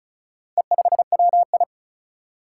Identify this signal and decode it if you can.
Morse code, heard as a keyed tone, about 36 wpm, 700 Hz. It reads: E5WI
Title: Four Challenging Letter-Number Combinations 35wpm